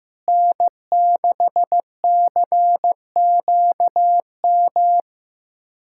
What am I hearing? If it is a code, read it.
Morse code, heard as a keyed tone, about 16 wpm, 700 Hz. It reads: N6CQM
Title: USA Callsign DE Callsign - 5 Characters - 15wpm